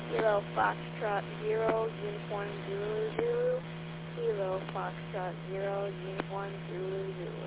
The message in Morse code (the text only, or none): none